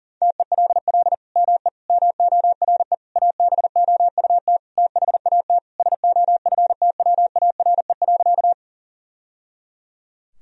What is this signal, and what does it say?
Morse code, heard as a keyed tone, about 40 wpm, 700 Hz. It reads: TELL ME MORE ABOUT THAT SOFTWARE.
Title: Sentences from Top 1000 Words - Review Part 3 - 40wpm